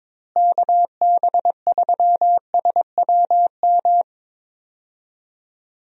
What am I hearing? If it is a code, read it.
Morse code, heard as a keyed tone, about 21 wpm, 700 Hz. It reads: KB3SWM
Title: USA Call Signs - 6 Characters - Repeated 3x - 22wpm